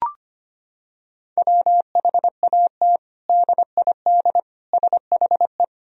Morse code, heard as a keyed tone, about 25 wpm, 700 Hz. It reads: WHAT DID SHE
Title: Sentences from Top 1000 Words - Review Part 2 - 25wpm